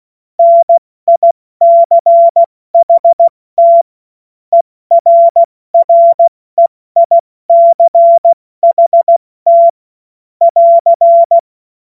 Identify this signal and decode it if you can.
Morse code, heard as a keyed tone, about 16 wpm, spaced as for 14 wpm, 680 Hz. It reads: NICHT ERREICHT <AR>
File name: Morsekurs / Witze / Witz 13. Juni 2023 70 F80